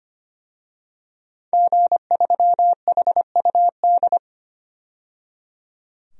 Morse code, heard as a keyed tone, about 25 wpm, 700 Hz. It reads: G3HUD
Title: DXCC-100 Call Signs - 5 Characters - 25wpm